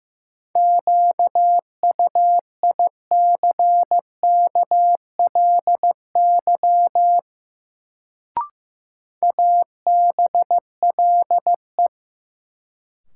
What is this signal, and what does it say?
Morse code, heard as a keyed tone, about 15 wpm, 700 Hz. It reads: QUICKLY ABLE
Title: Sentences from Top 700 Words - New Word Drill - 15wpm